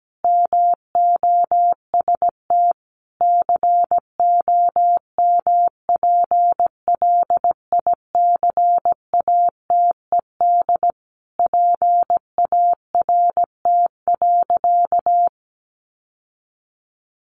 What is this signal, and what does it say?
Morse code, heard as a keyed tone, about 17 wpm, 700 Hz. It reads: MOST COMPLICATED PART.
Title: Sentences from Top 1200 Words 17wpm